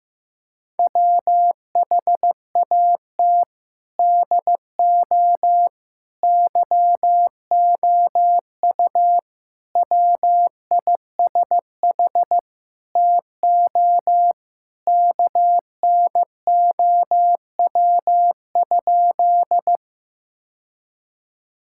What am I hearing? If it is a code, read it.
Morse code, heard as a keyed tone, about 16 wpm, 700 Hz. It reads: WHAT DO YOU WISH TO KNOW?
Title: Sentences from Top 1600 Words 15wpm